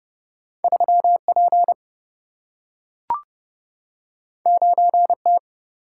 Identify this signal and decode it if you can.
Morse code, heard as a keyed tone, about 31 wpm, 700 Hz. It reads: 3 P 9 T
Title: Single-Letter Single-Number 30wpm